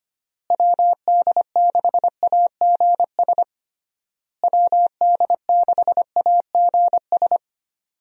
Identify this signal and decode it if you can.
Morse code, heard as a keyed tone, about 25 wpm, 700 Hz. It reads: WD6AGS WD6AGS
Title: USA Call Signs - 6 Characters - Repeated 3x - 25wpm